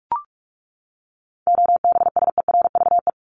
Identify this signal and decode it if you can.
Morse code, heard as a keyed tone, about 44 wpm, 700 Hz. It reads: OBSERVE
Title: ED Suffix - Top 100 Words with Base Words 45wpm